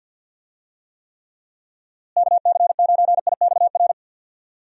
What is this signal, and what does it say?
Morse code, heard as a keyed tone, about 48 wpm, 700 Hz. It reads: KC9IXR
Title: USA Call Signs - 6 Characters - Repeated 3x - 50wpm